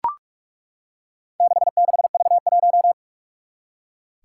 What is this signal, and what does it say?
Morse code, heard as a keyed tone, about 41 wpm, 700 Hz. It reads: 66V1
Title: Four Challenging Letter-Number Combinations 45wpm